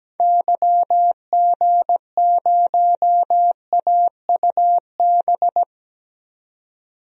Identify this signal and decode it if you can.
Morse code, heard as a keyed tone, about 17 wpm, 700 Hz. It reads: YG0AUB
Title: DXCC-100 Call Signs - 6 Characters - 17wpm